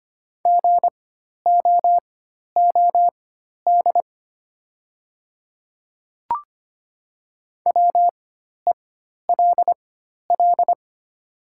Morse code, heard as a keyed tone, about 25 wpm, spaced as for 12 wpm, 700 Hz. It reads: GOOD WELL
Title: 4x Character Spacing - Top 200 Words - 25wpm